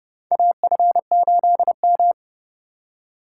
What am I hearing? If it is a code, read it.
Morse code, heard as a keyed tone, about 30 wpm, 700 Hz. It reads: AF8M
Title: USA Call Signs - 4 Characters - 3x Character Spacing - 30wpm